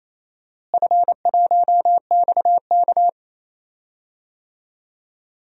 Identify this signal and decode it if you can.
Morse code, heard as a keyed tone, about 28 wpm, 700 Hz. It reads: F1XK
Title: DXCC-100 Call Signs - 4 Characters - Repeated 2x - 28wpm